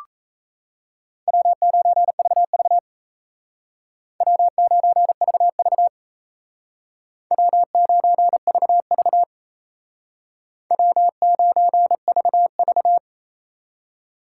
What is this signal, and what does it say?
Morse code, heard as a keyed tone, about 34 wpm, 700 Hz. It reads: W9VV W9VV W9VV W9VV
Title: Speed Racing - USA Call Signs 28wpm